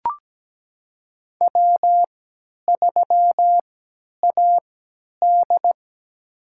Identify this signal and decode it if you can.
Morse code, heard as a keyed tone, about 17 wpm, spaced as for 10 wpm, 700 Hz. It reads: W3AD
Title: USA Call Signs - 3x Character Spacing - 17wpm